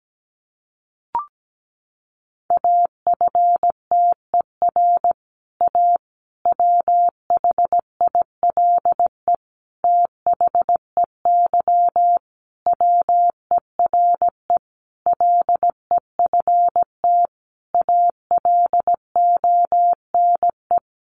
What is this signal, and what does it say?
Morse code, heard as a keyed tone, about 17 wpm, 700 Hz. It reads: AFTER A WHILE THEY WERE LEFT ALONE
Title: Sentences from Top 1000 Words 17wpm